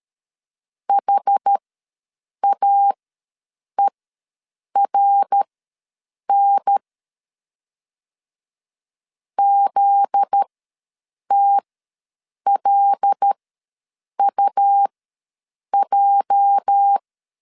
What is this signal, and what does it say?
Morse code, heard as a keyed tone, about 13 wpm, spaced as for 7 wpm, 800 Hz. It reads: HAERN ZTLUJ
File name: LEZIONI / 8 YJ45 / 8 YJ4 ws 12wpm 3249